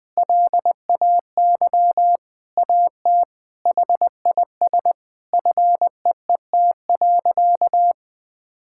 Morse code, heard as a keyed tone, about 21 wpm, 700 Hz. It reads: LAY AT HIS FEET.
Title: Sentences from Top 1000 Words - Review Part 5 - 20wpm